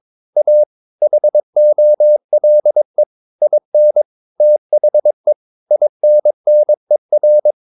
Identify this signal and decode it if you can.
Morse code, heard as a keyed tone, about 22 wpm, 600 Hz. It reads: A HOLE IN THE INNER
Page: Romanen Never af Ken Follet på engelsk.